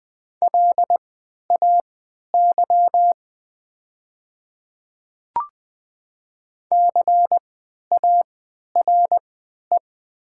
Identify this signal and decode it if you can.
Morse code, heard as a keyed tone, about 20 wpm, spaced as for 11 wpm, 700 Hz. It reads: LAY CARE
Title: 3x Character Spacing - Top 500 Words - 20wpm